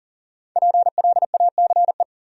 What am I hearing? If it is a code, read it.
Morse code, heard as a keyed tone, about 53 wpm, 700 Hz. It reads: PLACE